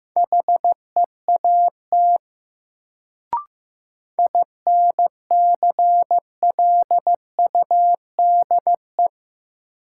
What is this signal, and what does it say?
Morse code, heard as a keyed tone, about 15 wpm, 700 Hz. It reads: HEAT INCLUDE
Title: Sentences from Top 1000 Words - Review Part 3 - New Word Drill - 15wpm